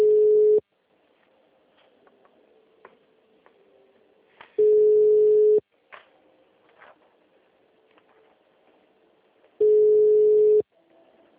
Ringback tone (Europe, 425 Hz).
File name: out- -2011- - - .51304